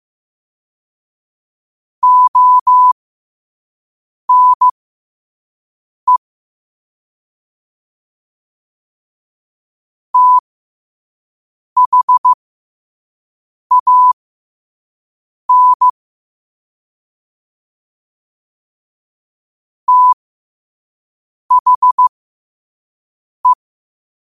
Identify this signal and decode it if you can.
Morse code, heard as a keyed tone, about 15 wpm, spaced as for 5 wpm, 1000 Hz. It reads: ONE THAN THE
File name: qotdfiles / 5wpm